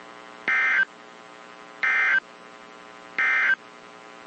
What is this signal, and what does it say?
Emergency Alert System (SAME) tones, heard as harsh buzzing bursts. It is heard from 0.4 to 3.6 s.
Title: WZ2523 - NOAA Weather Radio